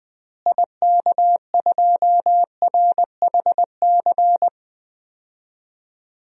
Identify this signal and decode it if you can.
Morse code, heard as a keyed tone, about 21 wpm, 690 Hz. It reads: IK2RHC
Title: DXCC-100 Call Signs - 6 Characters - Repeated 2x - 20wpm